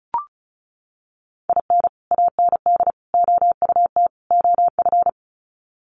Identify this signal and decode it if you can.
Morse code, heard as a keyed tone, about 34 wpm, 700 Hz. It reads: IN AND OUT OF
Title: Sets of 4 Words 35wpm